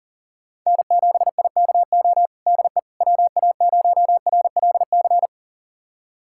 Morse code, heard as a keyed tone, about 40 wpm, 700 Hz. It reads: N7IKO DE WA0RLC
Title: USA Callsign DE Callsign 40wpm